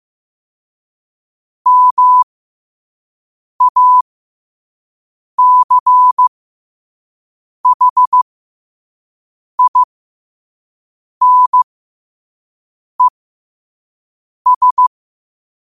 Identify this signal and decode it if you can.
Morse code, heard as a keyed tone, about 15 wpm, spaced as for 5 wpm, 1000 Hz. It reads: MACHINES
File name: qotdfiles / 5wpm